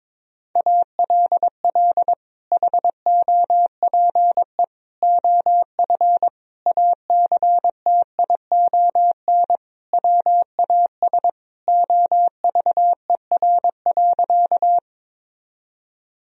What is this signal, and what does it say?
Morse code, heard as a keyed tone, about 22 wpm, 700 Hz. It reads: ALL HOPE OF ACTION WAS OVER.
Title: Sentences from Top 1000 Words - Review Part 4 - 22wpm